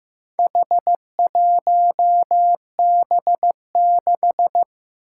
Morse code, heard as a keyed tone, about 15 wpm, 700 Hz. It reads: H1B6
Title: Four Challenging Letter-Number Combinations 15wpm